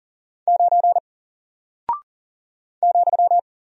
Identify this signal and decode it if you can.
Morse code, heard as a keyed tone, about 39 wpm, 700 Hz. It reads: 9 ,
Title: Single Character (Letter:Number:Punctuation:Prosign) 40wpm